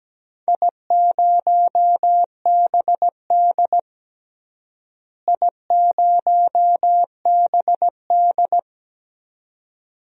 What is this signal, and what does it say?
Morse code, heard as a keyed tone, about 17 wpm, 700 Hz. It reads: I0BD I0BD
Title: DXCC-100 Call Signs - 4 Characters - Repeated 2x - 17wpm